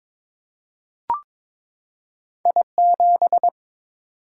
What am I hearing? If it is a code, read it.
Morse code, heard as a keyed tone, about 22 wpm, 690 Hz. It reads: I7
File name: practice / 22wpm / Two Letter-Number Combinations - 22wpm